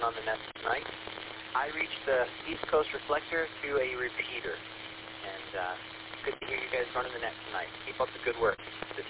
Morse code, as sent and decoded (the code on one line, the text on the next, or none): none